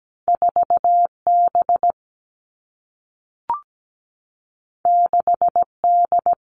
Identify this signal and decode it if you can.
Morse code, heard as a keyed tone, about 17 wpm, 700 Hz. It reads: 4B 6D
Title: All Cut Numbers - 2-Digit 17wpm